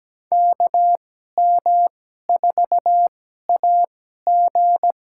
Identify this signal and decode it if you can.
Morse code, heard as a keyed tone, about 17 wpm, spaced as for 12 wpm, 700 Hz. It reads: KM4AG
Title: USA Call Signs - 5 Characters - 2x Character Spacing - 17wpm